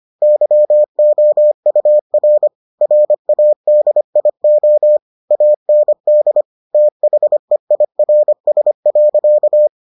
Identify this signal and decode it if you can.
Morse code, heard as a keyed tone, about 25 wpm, 600 Hz. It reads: YOUR RADIO AND THEIRS.